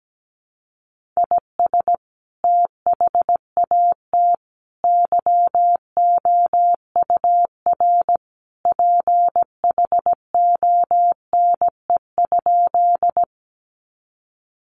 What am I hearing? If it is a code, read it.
Morse code, heard as a keyed tone, about 17 wpm, 700 Hz. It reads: IS THAT YOUR PHONE?
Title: Sentences from Top 1100 Words 17wpm